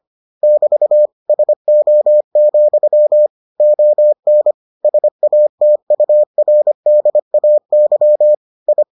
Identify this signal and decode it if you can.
Morse code, heard as a keyed tone, about 25 wpm, 600 Hz. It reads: <BT> SO, ON SATURDAY I